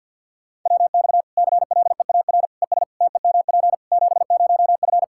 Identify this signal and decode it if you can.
Morse code, heard as a keyed tone, about 48 wpm, 700 Hz. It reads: WX CLEAR ES TEMP 70F
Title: K1USN SST QSO Practice 50wpm